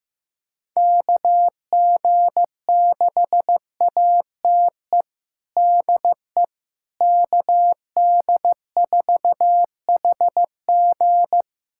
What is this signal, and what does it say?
Morse code, heard as a keyed tone, about 16 wpm, 700 Hz. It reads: KG6ATE DE KD4HG
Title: USA Callsign DE Callsign 15wpm